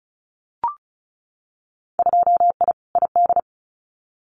Morse code, heard as a keyed tone, about 36 wpm, 700 Hz. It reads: 2I ID
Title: Winter Field Day Exchanges Encore 35wpm